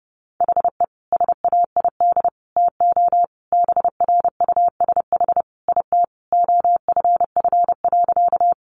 Morse code, heard as a keyed tone, about 30 wpm, 700 Hz. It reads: HE SAID TO BRUSH IT OFF.
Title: Sentences from Top 1100 Words 30wpm